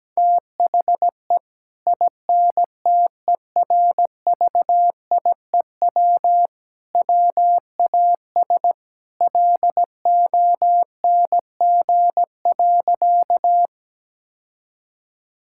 Morse code, heard as a keyed tone, about 17 wpm, 700 Hz. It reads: THE INTERVIEW WAS LONG.
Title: Sentences from Top 1600 Words 17wpm